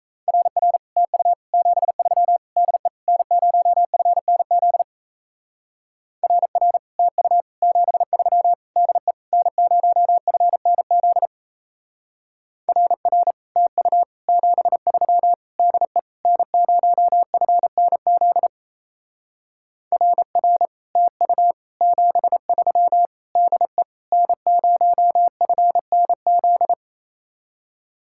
Morse code, heard as a keyed tone, about 36 wpm, 700 Hz. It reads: RR TU 73 DE N0FNZ RR TU 73 DE N0FNZ RR TU 73 DE N0FNZ RR TU 73 DE N0FNZ